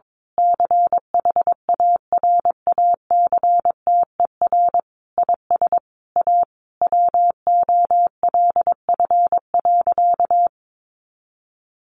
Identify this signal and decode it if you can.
Morse code, heard as a keyed tone, about 22 wpm, 700 Hz. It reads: CHARACTER IS A WOLF.
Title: Additional Easy Phrases 22wpm